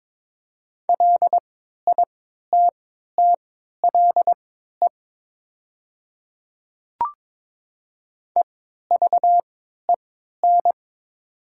Morse code, heard as a keyed tone, about 21 wpm, spaced as for 12 wpm, 700 Hz. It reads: LITTLE EVEN
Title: 3x Character Spacing - Top 200 Words - 22wpm